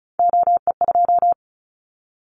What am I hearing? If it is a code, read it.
Morse code, heard as a keyed tone, about 34 wpm, 700 Hz. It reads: OE2
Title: Callsign Prefixes for Top 100 DXCC - 35wpm